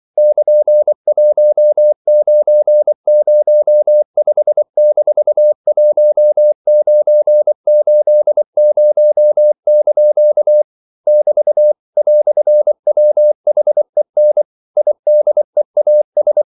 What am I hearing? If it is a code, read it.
Morse code, heard as a keyed tone, about 24 wpm, 600 Hz. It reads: <KN>1905-1980) <BT> "WHEN IDEAS
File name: CW / Practice / EnglishQuotations0324